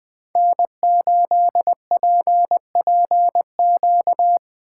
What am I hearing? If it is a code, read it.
Morse code, heard as a keyed tone, about 19 wpm, 700 Hz. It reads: N8PPQ
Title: Speed Racing - USA 1x3 Call Signs 17wpm